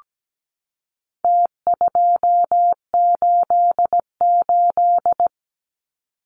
Morse code, heard as a keyed tone, about 17 wpm, 700 Hz. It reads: T288
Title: Numbers - Contest Style - 4-Digit 17wpm